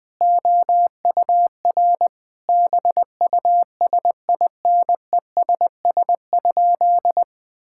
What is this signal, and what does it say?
Morse code, heard as a keyed tone, about 20 wpm, 700 Hz. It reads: OUR BUSINESS?
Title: Sentences from Top 1000 Words - Review Part 5 - 20wpm